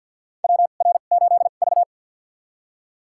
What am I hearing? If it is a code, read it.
Morse code, heard as a keyed tone, about 51 wpm, spaced as for 36 wpm, 700 Hz. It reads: WR8V
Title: USA Call Signs - 4 Characters - 2x Character Spacing - 50wpm